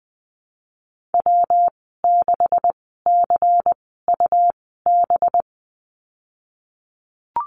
Morse code, heard as a keyed tone, about 20 wpm, spaced as for 15 wpm, 700 Hz. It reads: W6CUB
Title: 186 Lesson 30g - Letter b - Call Signs 20wpm